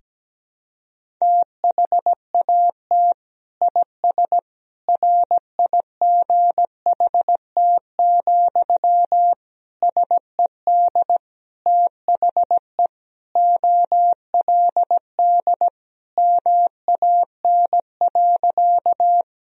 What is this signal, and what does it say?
Morse code, heard as a keyed tone, about 17 wpm, 700 Hz. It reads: THAT IS RIGHT, SED THE OLD MAN.
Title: Sentences from Top 200 Words and Abbreviations 17wpm